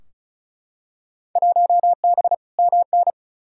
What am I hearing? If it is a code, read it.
Morse code, heard as a keyed tone, about 36 wpm, 690 Hz. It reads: 1B MN
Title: Field Day Exchanges Encore - 35wpm